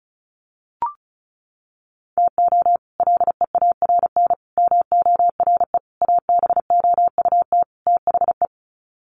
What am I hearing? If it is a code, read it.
Morse code, heard as a keyed tone, about 34 wpm, 700 Hz. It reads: TO LEARN MORE ABOUT THE
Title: Sets of 5 Words Encore2 35wpm